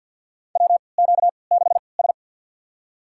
Morse code, heard as a keyed tone, about 48 wpm, spaced as for 28 wpm, 700 Hz. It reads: WQ6S
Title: USA Call Signs - 3x Character Spacing - 50wpm